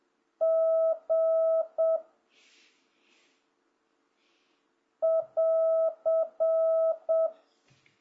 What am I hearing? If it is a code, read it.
Morse code, heard as a keyed tone, about 7 wpm, 1300 Hz. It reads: G <AR>